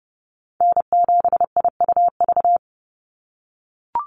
Morse code, heard as a keyed tone, about 30 wpm, 700 Hz. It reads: N7IUV
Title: USA Call Signs 30wpm